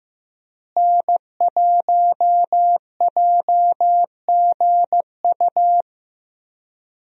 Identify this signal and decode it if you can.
Morse code, heard as a keyed tone, about 16 wpm, 700 Hz. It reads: N1JGU
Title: USA Call Signs - 5 Characters - Repeated 2x - 15wpm